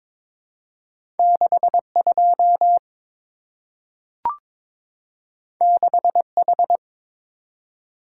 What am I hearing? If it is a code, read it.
Morse code, heard as a keyed tone, about 22 wpm, 700 Hz. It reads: 62 6H
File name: practice / 22wpm / Two Letter-Number Combinations - 22wpm